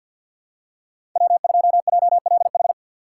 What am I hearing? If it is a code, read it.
Morse code, heard as a keyed tone, about 51 wpm, 700 Hz. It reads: W2JLH
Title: USA Call Signs - 5 Characters - 6x Character Spacing - 50wpm